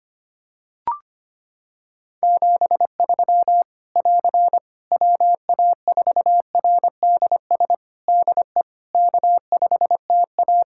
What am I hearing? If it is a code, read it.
Morse code, heard as a keyed tone, about 25 wpm, 700 Hz. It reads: 73 <AR> WA4RDS DE K5TA
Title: Practice QSO - Exchange 4 - Ending - 25wpm